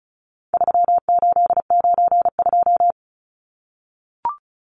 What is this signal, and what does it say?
Morse code, heard as a keyed tone, about 34 wpm, 700 Hz. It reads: 3892
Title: Numbers - 4-Digit 35wpm